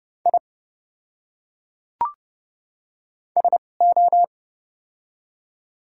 Morse code, heard as a keyed tone, about 30 wpm, spaced as for 5 wpm, 700 Hz. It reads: ISO
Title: 2x Character Spacing - Top 300 Words - 30wpm